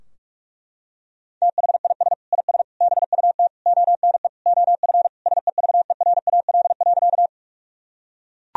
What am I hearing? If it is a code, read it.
Morse code, heard as a keyed tone, about 41 wpm, 700 Hz. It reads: THIS IS BUT ONE OF SEVERAL.
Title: Sentences from Top 1000 Words - Review Part 2 - 45wpm